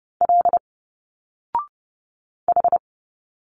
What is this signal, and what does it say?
Morse code, heard as a keyed tone, about 30 wpm, 700 Hz. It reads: L H
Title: Single Letters 30wpm